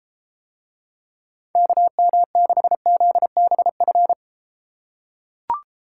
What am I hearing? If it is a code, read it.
Morse code, heard as a keyed tone, about 33 wpm, 700 Hz. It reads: KM6ZBF